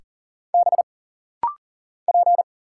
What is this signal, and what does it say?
Morse code, heard as a keyed tone, about 40 wpm, 700 Hz. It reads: B P
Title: Single Letter-Number - Rapid-Fire 40wpm